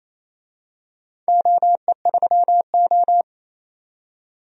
Morse code, heard as a keyed tone, about 28 wpm, 700 Hz. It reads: OE3O
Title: DXCC-100 Call Signs - 4 Characters - 28wpm